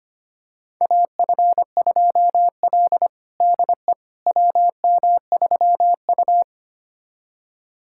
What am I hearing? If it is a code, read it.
Morse code, heard as a keyed tone, about 25 wpm, 700 Hz. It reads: AF2L DE WM3U
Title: USA Callsign DE Callsign - 4 Characters - 25wpm